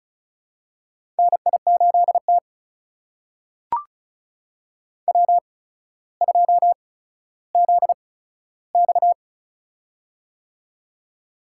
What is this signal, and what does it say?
Morse code, heard as a keyed tone, about 36 wpm, spaced as for 15 wpm, 700 Hz. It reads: NI8T W2ZX
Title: USA Call Signs - 4 Characters - 8x Character Spacing - 35wpm